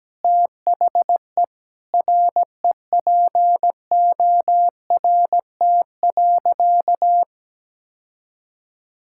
Morse code, heard as a keyed tone, about 17 wpm, 700 Hz. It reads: THE REPORT.
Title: Sentences from Top 1000 Words - Review Part 6 - 17wpm